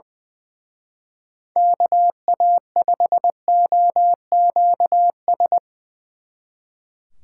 Morse code, heard as a keyed tone, about 22 wpm, 700 Hz. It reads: KA5OQS
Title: Speed Racing - USA 2x3 Call Signs 20wpm